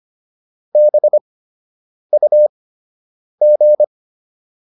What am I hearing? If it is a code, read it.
Morse code, heard as a keyed tone, about 25 wpm, spaced as for 8 wpm, 600 Hz. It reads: BUG